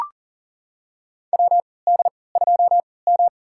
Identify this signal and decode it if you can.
Morse code, heard as a keyed tone, about 40 wpm, spaced as for 23 wpm, 700 Hz. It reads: WD2M
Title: USA Call Signs - 4 Characters - 3x Character Spacing - 40wpm